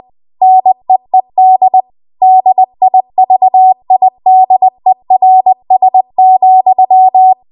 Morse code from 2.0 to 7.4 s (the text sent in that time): DIVIDERS,